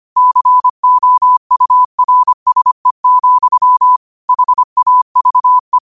Morse code, heard as a keyed tone, about 25 wpm, 1000 Hz. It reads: COURSE, HAVE
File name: qotdfiles / 25wpm